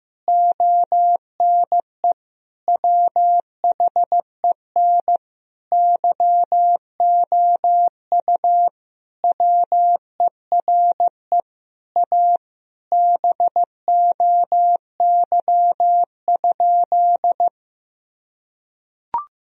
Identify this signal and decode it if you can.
Morse code, heard as a keyed tone, about 15 wpm, 700 Hz. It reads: ONE WHEN YOU WERE A BOY?
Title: Sentences from Top 1000 Words - Review Part 1 - 15wpm